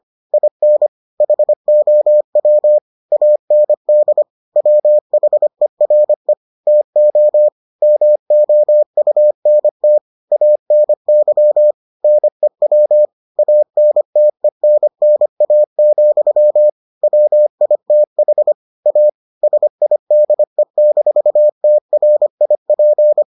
Morse code, heard as a keyed tone, about 25 wpm, 600 Hz. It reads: IN HOW AND WHERE TO MOUNT ANY NEW ANTENNA, WITH A SIDE-TRIP